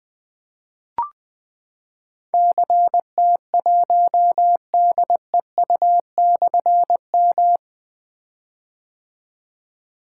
Morse code, heard as a keyed tone, about 20 wpm, 700 Hz. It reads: CT1DEU/M
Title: Difficult Call Signs 20wpm